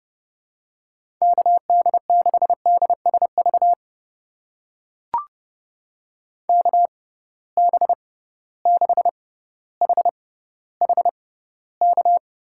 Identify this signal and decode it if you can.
Morse code, heard as a keyed tone, about 30 wpm, spaced as for 15 wpm, 700 Hz. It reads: KD6DSV KB6HHK
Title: USA Call Signs - 6 Characters - 6x Character Spacing - 30wpm